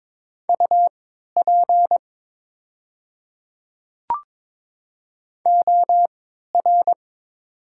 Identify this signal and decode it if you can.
Morse code, heard as a keyed tone, about 22 wpm, spaced as for 12 wpm, 700 Hz. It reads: UP OR
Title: 3x Character Spacing - Top 200 Words - 22wpm